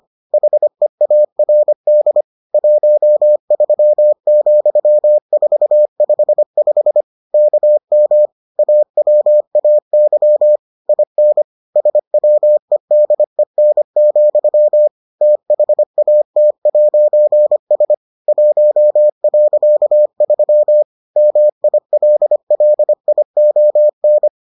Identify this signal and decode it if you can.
Morse code, heard as a keyed tone, about 25 wpm, 600 Hz. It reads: HEARD 13,455 KM AWAY IN SWEDEN, THAT'S 1.3 MILLION